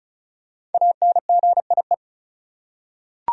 Morse code, heard as a keyed tone, about 34 wpm, 700 Hz. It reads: ANGIE
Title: USA Common Names - Max 5 Letters - 35wpm